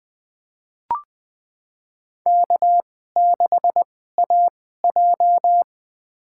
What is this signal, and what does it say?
Morse code, heard as a keyed tone, about 20 wpm, spaced as for 14 wpm, 700 Hz. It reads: K6AJ
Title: USA Call Signs - 4 Characters - 2x Character Spacing - 20wpm